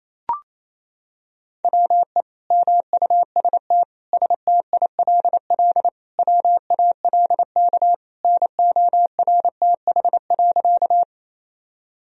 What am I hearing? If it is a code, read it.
Morse code, heard as a keyed tone, about 28 wpm, 700 Hz. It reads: WE MUST STILL WALK NORTH.
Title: Sentences from Top 1000 Words - Review Part 1 - 28wpm